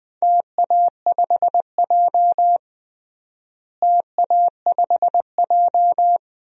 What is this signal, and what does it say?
Morse code, heard as a keyed tone, about 20 wpm, 700 Hz. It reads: TA5J TA5J
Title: DXCC-100 Call Signs - 4 Characters - Repeated 3x - 20wpm